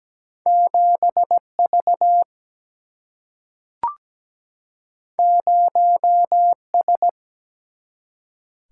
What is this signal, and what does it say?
Morse code, heard as a keyed tone, about 17 wpm, 700 Hz. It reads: 7 V 0 S
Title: Single-Letter Single-Number 17wpm